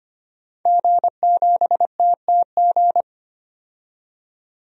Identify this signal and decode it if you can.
Morse code, heard as a keyed tone, about 25 wpm, 700 Hz. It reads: G7TTG
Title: DXCC-100 Call Signs - 5 Characters - 25wpm